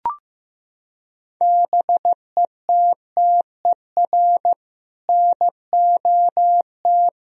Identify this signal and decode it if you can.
Morse code, heard as a keyed tone, about 15 wpm, 700 Hz. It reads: BETTER NOT
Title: Sentences from Top 1000 Words - Review Part 2 - 15wpm